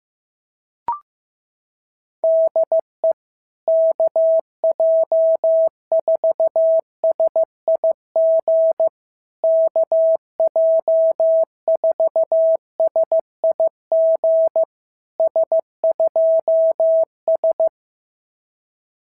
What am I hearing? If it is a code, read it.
Morse code, heard as a keyed tone, about 15 wpm, 650 Hz. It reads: DE KJ4SIG KJ4SIG S2S
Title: SOTA QSO Practice 15wpm